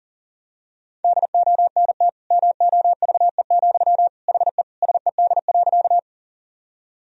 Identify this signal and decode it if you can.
Morse code, heard as a keyed tone, about 40 wpm, 700 Hz. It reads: DONT MOVE, HE SED.
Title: Sentences from Top 300 Words and Abbreviations 40wpm